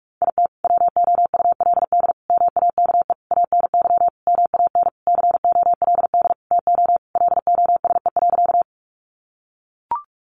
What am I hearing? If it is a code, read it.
Morse code, heard as a keyed tone, about 44 wpm, 700 Hz. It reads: IT WOULD MAKE ANY MAN COLD TO LOSE.
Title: Sentences from Top 1100 Words 45wpm